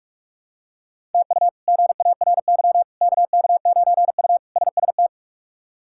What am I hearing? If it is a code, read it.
Morse code, heard as a keyed tone, about 46 wpm, 700 Hz. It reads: TU GARY KK9U SST
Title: K1USN SST QSO Practice - 45wpm